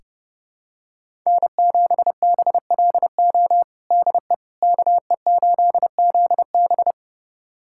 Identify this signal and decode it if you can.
Morse code, heard as a keyed tone, about 32 wpm, 700 Hz. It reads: N7BLO DE KE8ZB
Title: USA Callsign DE Callsign - 5 Characters - 30wpm